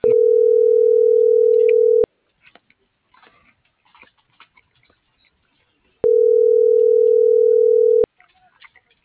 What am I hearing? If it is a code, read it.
Ringback tone (North America), 440 + 480 Hz.
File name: CENTRAL GOMSA / out cisneros.hector